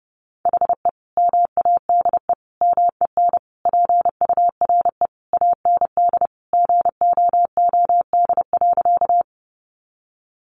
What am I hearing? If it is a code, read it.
Morse code, heard as a keyed tone, about 30 wpm, 700 Hz. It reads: HE MADE MEN PURE AND GOOD.
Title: Sentences from Top 1500 Words 30wpm